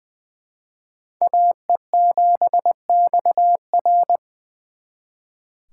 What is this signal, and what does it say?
Morse code, heard as a keyed tone, about 23 wpm, 700 Hz. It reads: AE7XR
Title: Speed Racing - USA 2x2 Call Signs 20wpm